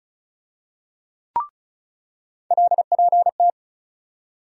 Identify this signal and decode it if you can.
Morse code, heard as a keyed tone, about 34 wpm, 700 Hz. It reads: LPT
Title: USA Call Sign Postfixes - 3 Letters 35wpm